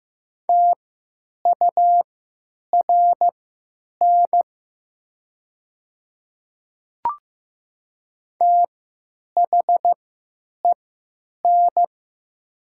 Morse code, heard as a keyed tone, about 15 wpm, spaced as for 9 wpm, 700 Hz. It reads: TURN THEN
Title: 3x Character Spacing - Top 200 Words - 15wpm